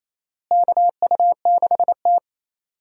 Morse code, heard as a keyed tone, about 28 wpm, 700 Hz. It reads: KU6T
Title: USA Call Signs - 4 Characters - Repeated 2x - 28wpm